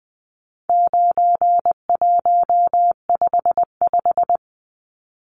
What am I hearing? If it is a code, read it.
Morse code, heard as a keyed tone, about 20 wpm, 700 Hz. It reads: 9155
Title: Numbers - 4-Digit 20wpm